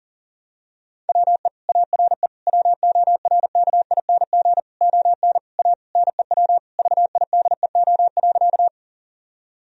Morse code, heard as a keyed tone, about 37 wpm, 700 Hz. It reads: WE ARE WORKING ON A NEW VIDEO.
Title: Sentences from Top 1000 Words - Review Part 4 - 40wpm